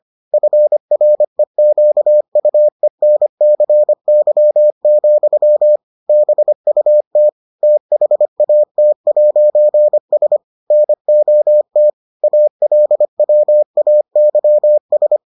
Morse code, heard as a keyed tone, about 25 wpm, 600 Hz. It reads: FREQUENCY, BUT THAT'S NOT ALWAYS